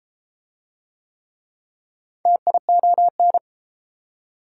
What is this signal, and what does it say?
Morse code, heard as a keyed tone, about 33 wpm, 700 Hz. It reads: TION